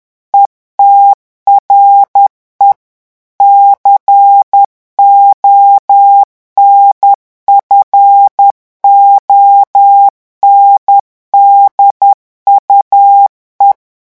Morse code, heard as a keyed tone, about 10 wpm, 800 Hz. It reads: ETRE CONFONDUE
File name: morse mp3 / revision francais 10 mots minute